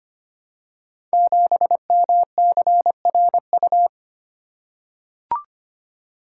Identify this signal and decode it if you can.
Morse code, heard as a keyed tone, about 25 wpm, 700 Hz. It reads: 7MCRU
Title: Code Groups - 5 Letter:Numbers - 25wpm